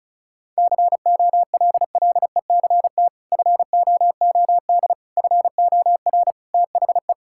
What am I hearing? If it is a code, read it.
Morse code, heard as a keyed tone, about 34 wpm, 700 Hz. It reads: COLLECT FOOD FOR THE
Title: Sentences from Top 1600 Words 35wpm